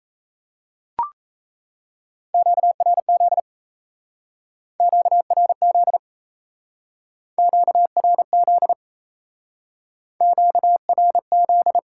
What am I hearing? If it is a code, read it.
Morse code, heard as a keyed tone, about 35 wpm, 700 Hz. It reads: QRZ QRZ QRZ QRZ